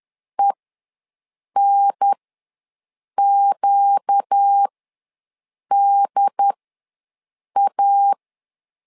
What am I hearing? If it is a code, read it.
Morse code, heard as a keyed tone, about 10 wpm, spaced as for 6 wpm, 800 Hz. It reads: ENQDA
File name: LEZIONI / 4 Q ws 10wpm 2306